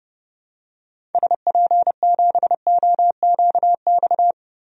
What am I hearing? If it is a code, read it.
Morse code, heard as a keyed tone, about 30 wpm, 700 Hz. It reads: SP7OQX
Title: DXCC-100 Call Signs - 6 Characters - Repeated 2x - 30wpm